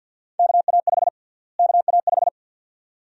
Morse code, heard as a keyed tone, about 51 wpm, 700 Hz. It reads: KA5 KA5
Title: USA Call Sign Prefixes - Repeated 2x - 50wpm